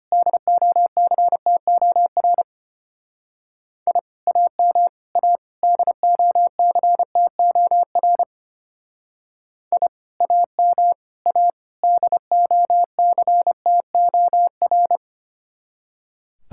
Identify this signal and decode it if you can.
Morse code, heard as a keyed tone, about 31 wpm, 700 Hz. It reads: DOCTOR I AM A DOCTOR I AM A DOCTOR
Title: Speed-Racing Common QSO Phrases Encore - 25wpm